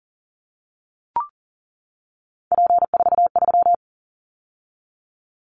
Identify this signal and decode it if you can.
Morse code, heard as a keyed tone, about 40 wpm, 700 Hz. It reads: P43
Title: Callsign Prefixes for Top 100 DXCC - 40wpm